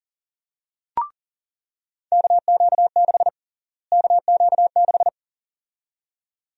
Morse code, heard as a keyed tone, about 40 wpm, 700 Hz. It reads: KQ6 KQ6
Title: USA Call Sign Prefixes - Repeated 2x - 40wpm